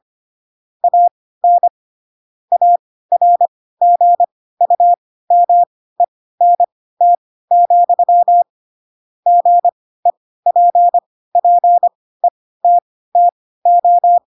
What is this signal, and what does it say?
Morse code, heard as a keyed tone, about 25 wpm, spaced as for 16 wpm, 700 Hz. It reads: AN ARGUMENT, GEPPETTO
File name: Pinocchio / Pinocchio ch3 10 25 ik1hge